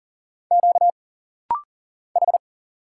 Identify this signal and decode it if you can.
Morse code, heard as a keyed tone, about 39 wpm, 700 Hz. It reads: Q H
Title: Single Letter-Number Alt - Rapid-Fire 40wpm